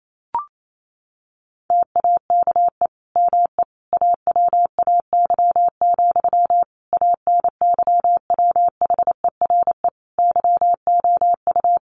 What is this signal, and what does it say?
Morse code, heard as a keyed tone, about 28 wpm, 700 Hz. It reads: TAKE ME AWAY, ANYWHERE YOU
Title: Sentences from Top 1400 Words 28wpm